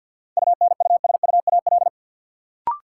Morse code, heard as a keyed tone, about 53 wpm, 700 Hz. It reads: UNUSUAL
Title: Sentences from Top 1400 Words - New Word Drill Alt - 50wpm